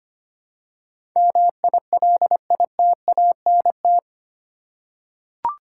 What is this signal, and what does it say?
Morse code, heard as a keyed tone, about 25 wpm, 700 Hz. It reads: MILITANT
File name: Eight Letter Words - 25wpm